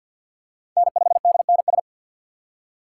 Morse code, heard as a keyed tone, about 46 wpm, 700 Hz. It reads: N5DNS
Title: USA Call Signs - 5 Characters - 2x Character Spacing - 50wpm